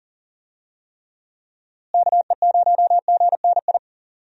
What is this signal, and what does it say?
Morse code, heard as a keyed tone, about 40 wpm, 700 Hz. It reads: KE0GNI
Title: USA Call Signs - 6 Characters - Repeated 2x - 40wpm